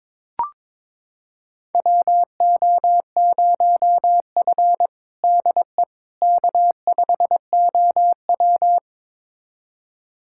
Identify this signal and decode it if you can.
Morse code, heard as a keyed tone, about 22 wpm, 700 Hz. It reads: WO0F DE K5OW
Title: USA Callsign DE Callsign - 4 Characters - 22wpm